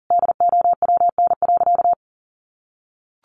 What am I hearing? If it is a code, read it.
Morse code, heard as a keyed tone, about 37 wpm, 700 Hz. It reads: DOWN.
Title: Sentences from Top 500 Words 40wpm